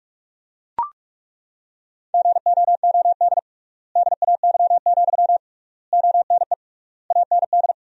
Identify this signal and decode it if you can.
Morse code, heard as a keyed tone, about 42 wpm, 700 Hz. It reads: GOOD DAY, ONE AND
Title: 3x Word-Spacing - Sentences from Top 200 Words - 45wpm